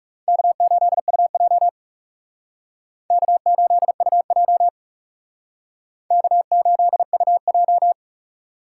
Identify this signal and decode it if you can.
Morse code, heard as a keyed tone, about 39 wpm, 700 Hz. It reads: K8UJ K8UJ K8UJ